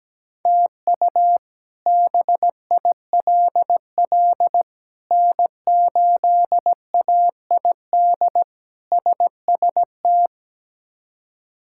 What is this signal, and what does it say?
Morse code, heard as a keyed tone, about 17 wpm, 700 Hz. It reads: TU BILL N8AID SST
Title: K1USN SST QSO Practice - 17wpm